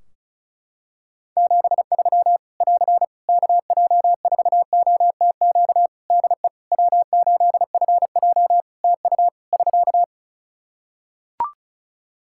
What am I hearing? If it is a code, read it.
Morse code, heard as a keyed tone, about 34 wpm, 700 Hz. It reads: 73 <AR> KJ4OTQ DE W8FJ TU <SK>
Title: Practice QSO - Exchange 4 - Ending - 35wpm